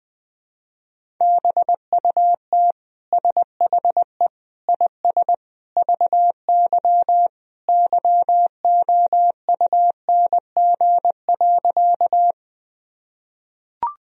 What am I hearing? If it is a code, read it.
Morse code, heard as a keyed tone, about 20 wpm, 700 Hz. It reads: BUT SHE IS VY YOUNG.
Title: Sentences from Top 400 Words and Abbreviations 20wpm